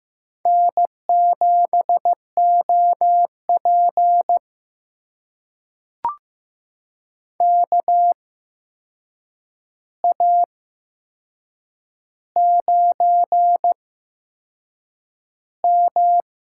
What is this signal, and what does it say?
Morse code, heard as a keyed tone, about 15 wpm, spaced as for 6 wpm, 700 Hz. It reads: N7OP KA9M
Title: USA Call Signs - 4 Characters - 8x Character Spacing - 15wpm